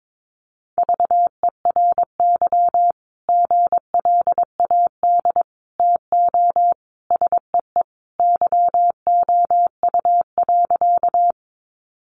Morse code, heard as a keyed tone, about 22 wpm, 700 Hz. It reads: VERY GLAD TO SEE YOU.
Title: Sentences from Top 800 Words 22wpm